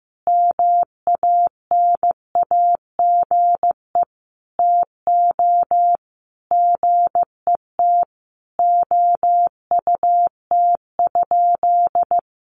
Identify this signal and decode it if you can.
Morse code, heard as a keyed tone, about 15 wpm, 700 Hz. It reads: MANAGE TO GET OUT?
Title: Sentences from Top 1400 Words 15wpm